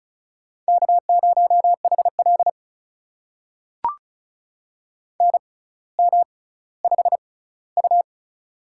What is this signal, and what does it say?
Morse code, heard as a keyed tone, about 36 wpm, spaced as for 18 wpm, 700 Hz. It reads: K0HL NM5U
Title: USA Call Signs - 4 Characters - 6x Character Spacing - 35wpm